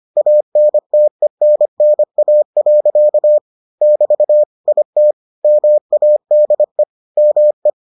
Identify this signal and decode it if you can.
Morse code, heard as a keyed tone, about 25 wpm, 600 Hz. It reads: ANTENNA. <BT> IT MADE ME